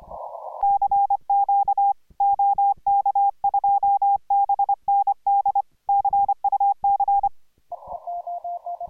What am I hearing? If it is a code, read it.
Morse code, heard as a keyed tone, about 24 wpm, 810 Hz. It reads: CQ OK2BND CUC